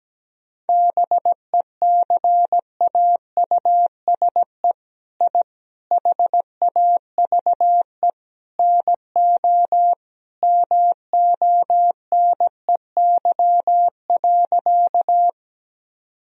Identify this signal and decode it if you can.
Morse code, heard as a keyed tone, about 17 wpm, 700 Hz. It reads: BECAUSE I HAVE NO MONEY.
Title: Sentences from Top 1000 Words - Review Part 3 - 17wpm